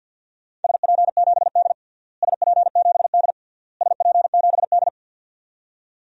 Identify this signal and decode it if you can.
Morse code, heard as a keyed tone, about 51 wpm, 700 Hz. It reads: SP7D SP7D SP7D
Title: DXCC-100 Call Signs - 4 Characters - Repeated 3x - 50wpm